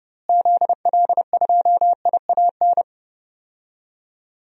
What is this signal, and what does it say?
Morse code, heard as a keyed tone, about 30 wpm, 700 Hz. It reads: ZL2IAN
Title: DXCC-100 Call Signs - 6 Characters - Repeated 3x - 30wpm